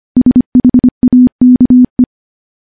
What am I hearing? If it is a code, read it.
Morse code, heard as a keyed tone, about 25 wpm, 250 Hz. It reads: SHAKE
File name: morse / morse-training-25WPM